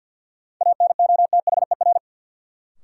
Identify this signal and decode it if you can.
Morse code, heard as a keyed tone, about 53 wpm, 700 Hz. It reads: ANOTHER
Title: Sets of 5 Words Encore - New Word Drill - 50wpm